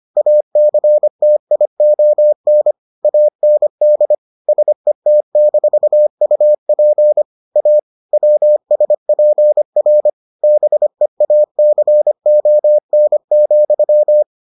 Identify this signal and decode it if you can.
Morse code, heard as a keyed tone, about 25 wpm, 600 Hz. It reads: ACTION AND SET-UP A WSPR BEACON,